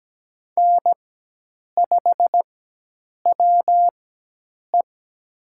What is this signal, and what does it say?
Morse code, heard as a keyed tone, about 17 wpm, spaced as for 8 wpm, 700 Hz. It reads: N5WE
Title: USA Call Signs - 4x Character Spacing - 17wpm